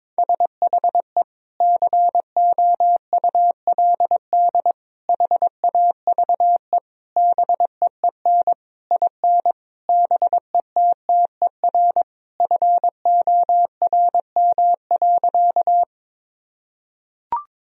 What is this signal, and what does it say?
Morse code, heard as a keyed tone, about 22 wpm, 700 Hz. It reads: SHE COULD HAVE BEEN IN BETTER FORM.
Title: Sentences from Top 1000 Words - Review Part 1 - 22wpm